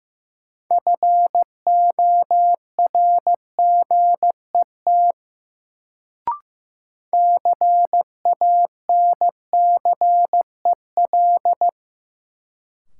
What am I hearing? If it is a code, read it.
Morse code, heard as a keyed tone, about 15 wpm, 700 Hz. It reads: FORGET CANCEL
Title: Sentences from Top 1200 Words - New Word Drill Alt - 15wpm